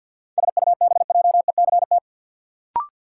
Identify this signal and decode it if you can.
Morse code, heard as a keyed tone, about 53 wpm, 700 Hz. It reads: SUBJECT
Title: Sentences from Top 800 Words - New Word Drill - 50wpm